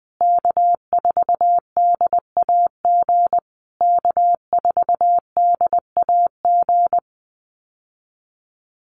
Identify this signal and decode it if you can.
Morse code, heard as a keyed tone, about 20 wpm, 700 Hz. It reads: K4DAG K4DAG
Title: USA Call Signs Repeated 3x 20wpm